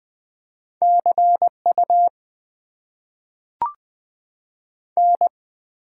Morse code, heard as a keyed tone, about 20 wpm, 700 Hz. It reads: CU N
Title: Top 90 QSO Elements - 6x Character Spacing - 20wpm